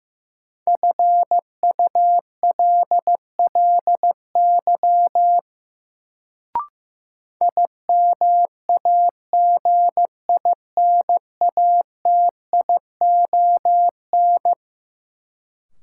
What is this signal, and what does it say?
Morse code, heard as a keyed tone, about 15 wpm, 700 Hz. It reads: FULLY IMAGINATION
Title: Sentences from Top 1300 Words - New Word Drill Alt - 15wpm